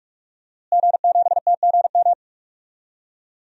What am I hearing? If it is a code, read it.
Morse code, heard as a keyed tone, about 46 wpm, 700 Hz. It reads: G7TGM
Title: DXCC-100 Call Signs - 5 Characters - Repeated 2x - 45wpm